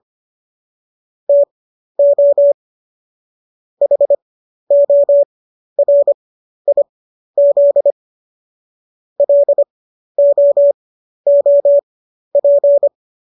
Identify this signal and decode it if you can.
Morse code, heard as a keyed tone, about 25 wpm, spaced as for 12 wpm, 570 Hz. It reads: TO HORIZ LOOP